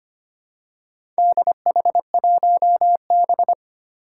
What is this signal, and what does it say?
Morse code, heard as a keyed tone, about 25 wpm, 700 Hz. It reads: DH1B
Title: Four Challenging Letter-Number Combinations 25wpm